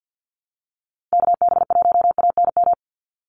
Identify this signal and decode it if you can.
Morse code, heard as a keyed tone, about 50 wpm, 700 Hz. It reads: KB1ANM
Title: USA Call Signs 50wpm